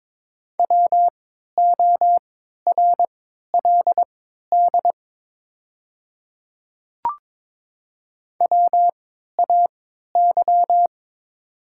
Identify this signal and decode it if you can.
Morse code, heard as a keyed tone, about 22 wpm, spaced as for 12 wpm, 700 Hz. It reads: WORLD WAY
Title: 3x Character Spacing - Top 300 Words - 22wpm